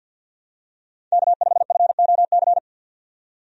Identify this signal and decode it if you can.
Morse code, heard as a keyed tone, about 52 wpm, 700 Hz. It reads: K5FOC
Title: USA Call Signs - 5 Characters - 50wpm